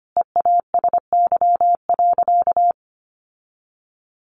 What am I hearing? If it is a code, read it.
Morse code, heard as a keyed tone, about 25 wpm, 700 Hz. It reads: EASY.
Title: Sentences from Top 700 Words 25wpm